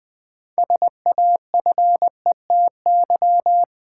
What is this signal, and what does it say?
Morse code, heard as a keyed tone, about 20 wpm, 700 Hz. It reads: SAFETY
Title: Build A Word - 2 Letters At A Time - 6-Letter Words - 20wpm